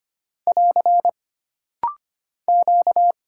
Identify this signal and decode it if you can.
Morse code, heard as a keyed tone, about 25 wpm, 700 Hz. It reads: <AR> Q
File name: Single Character (Letter:Number:Punctuation:Prosign) - Rapid-Fire - 25wpm